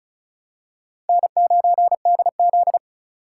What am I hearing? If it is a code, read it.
Morse code, heard as a keyed tone, about 36 wpm, 700 Hz. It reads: N9DZ
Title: USA Call Signs - 4 Characters - 2x Character Spacing - 35wpm